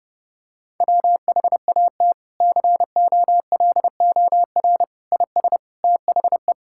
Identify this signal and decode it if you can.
Morse code, heard as a keyed tone, about 30 wpm, 700 Hz. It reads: WHAT COLOR IS THE
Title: Sentences from Top 1000 Words - Review Part 6 - 30wpm